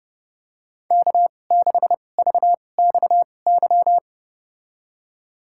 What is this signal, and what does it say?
Morse code, heard as a keyed tone, about 30 wpm, spaced as for 22 wpm, 700 Hz. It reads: K6VXY
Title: USA Call Signs - 5 Characters - 2x Character Spacing - 30wpm